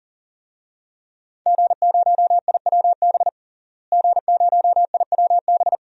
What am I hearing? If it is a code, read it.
Morse code, heard as a keyed tone, about 40 wpm, 700 Hz. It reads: G0IWB G0IWB
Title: DXCC-100 Call Signs - 5 Characters - Repeated 2x - 40wpm